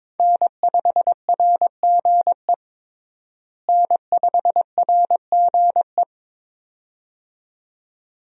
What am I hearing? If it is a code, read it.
Morse code, heard as a keyed tone, about 21 wpm, 700 Hz. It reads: N5RGE N5RGE
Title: USA Call Signs - 5 Characters - Repeated 2x - 22wpm